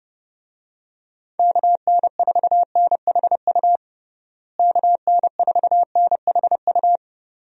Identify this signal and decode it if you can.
Morse code, heard as a keyed tone, about 30 wpm, 700 Hz. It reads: KN4NHU KN4NHU
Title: USA Call Signs - 6 Characters - Repeated 2x - 30wpm